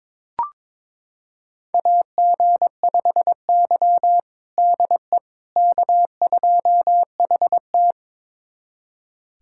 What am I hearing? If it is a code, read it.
Morse code, heard as a keyed tone, about 22 wpm, 700 Hz. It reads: AG5Y DE K2HT
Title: USA Callsign DE Callsign - 4 Characters - 22wpm